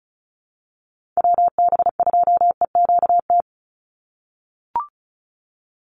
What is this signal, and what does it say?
Morse code, heard as a keyed tone, about 36 wpm, 700 Hz. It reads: WB2EQT
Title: USA Call Signs 35wpm